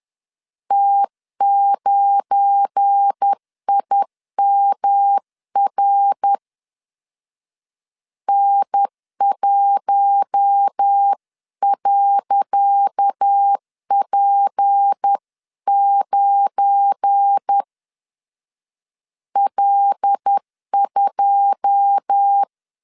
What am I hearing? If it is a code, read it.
Morse code, heard as a keyed tone, about 10 wpm, 800 Hz. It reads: T9IMR N1.P9 L2
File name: LEZIONI / 10 SIGNS / 10C DOT 10wpm 2940